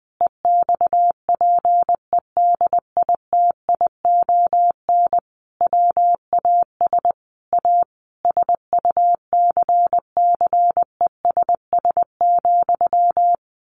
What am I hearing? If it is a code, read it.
Morse code, heard as a keyed tone, about 20 wpm, 700 Hz. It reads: EXPEDITION WAS A SUCCESS,
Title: A Sound Of Thunder 20wpm